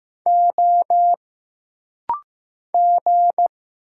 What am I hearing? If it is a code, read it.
Morse code, heard as a keyed tone, about 15 wpm, 700 Hz. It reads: O G
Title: Single Letters - Rapid-Fire 15wpm